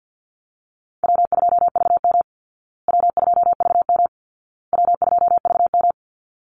Morse code, heard as a keyed tone, about 50 wpm, 700 Hz. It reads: W2VM W2VM W2VM
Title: Call Signs Repeated 2x - 50wpm